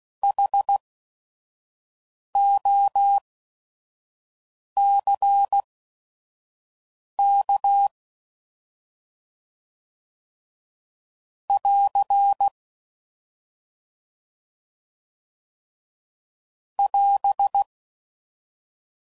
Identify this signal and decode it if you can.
Morse code, heard as a keyed tone, about 16 wpm, spaced as for 5 wpm, 800 Hz. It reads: HOCK <AR> <AS>
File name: Prefixes / By First Letter / Hs